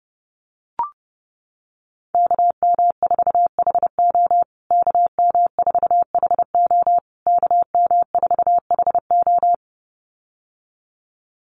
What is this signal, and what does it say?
Morse code, heard as a keyed tone, about 30 wpm, 700 Hz. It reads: KM4HO KM4HO KM4HO
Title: USA Call Signs Repeated 3x 30wpm